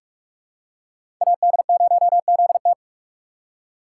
Morse code, heard as a keyed tone, about 46 wpm, 700 Hz. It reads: AD0ZT
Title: USA Call Signs - 4x Character Spacing - 45wpm